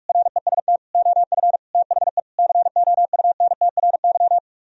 Morse code, heard as a keyed tone, about 44 wpm, 700 Hz. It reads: REST OF THE COUNTRY